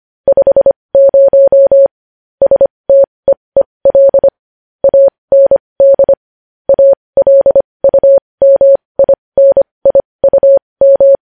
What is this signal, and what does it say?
Morse code, heard as a keyed tone, about 25 wpm, spaced as for 20 wpm, 550 Hz. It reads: 50 STEEL AND ALUMINIUM
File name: podcast / News headlines-1x-25x20